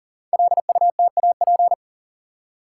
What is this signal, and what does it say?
Morse code, heard as a keyed tone, about 39 wpm, 700 Hz. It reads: LUTAP
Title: Code Groups - 5 Letters - 40wpm